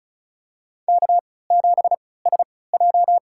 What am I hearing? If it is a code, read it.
Morse code, heard as a keyed tone, about 36 wpm, spaced as for 20 wpm, 700 Hz. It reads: K7SJ
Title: USA Call Signs - 4 Characters - 3x Character Spacing - 35wpm